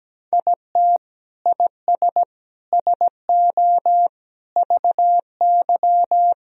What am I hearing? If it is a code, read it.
Morse code, heard as a keyed tone, about 17 wpm, 700 Hz. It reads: IT IS SO VY
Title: Sentences from Top 300 Words and Abbreviations 17wpm